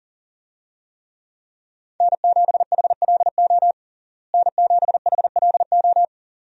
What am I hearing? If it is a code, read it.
Morse code, heard as a keyed tone, about 37 wpm, 700 Hz. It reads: N7HLO N7HLO
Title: USA Call Signs - 5 Characters - Repeated 2x - 40wpm